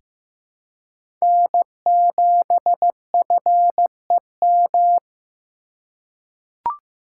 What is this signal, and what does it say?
Morse code, heard as a keyed tone, about 15 wpm, 700 Hz. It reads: N7FEM
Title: USA Call Signs - 4x Character Spacing - 15wpm